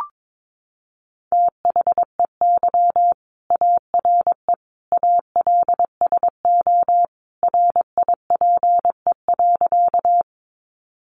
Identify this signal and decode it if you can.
Morse code, heard as a keyed tone, about 22 wpm, 700 Hz. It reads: THEY ARE ALSO RIPE.
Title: Additional Easy Phrases 22wpm